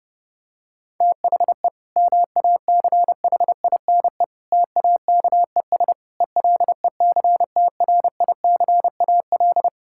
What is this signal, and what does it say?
Morse code, heard as a keyed tone, about 30 wpm, 700 Hz. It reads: THE MACHINE TAKES ELECTRICAL
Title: Sentences from Top 1000 Words - Review Part 6 - 30wpm